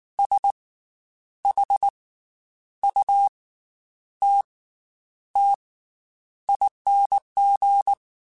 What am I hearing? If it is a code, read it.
Morse code, heard as a keyed tone, about 19 wpm, spaced as for 9 wpm, 800 Hz. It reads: SHUTTING
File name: Special INGs / Special INGs-mp3s / Slow ing @18